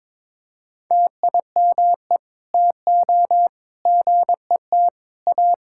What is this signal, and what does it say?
Morse code, heard as a keyed tone, about 22 wpm, 700 Hz. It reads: TIME TO GET A
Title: Sentences from Top 1600 Words 22wpm